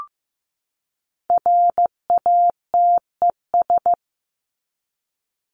Morse code, heard as a keyed tone, about 15 wpm, 700 Hz. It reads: RATES
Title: ES:S Suffix - Top 100 Words with Base Words 15wpm